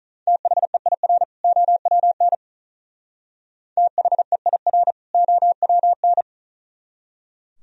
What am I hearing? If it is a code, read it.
Morse code, heard as a keyed tone, about 43 wpm, 700 Hz. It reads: THEIR OWN THEIR OWN
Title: Speed-Racing - Sets of 2 Words 35wpm